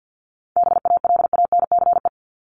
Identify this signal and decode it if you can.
Morse code, heard as a keyed tone, about 50 wpm, 700 Hz. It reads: BALANCE
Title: Top 751-1000 Words 50wpm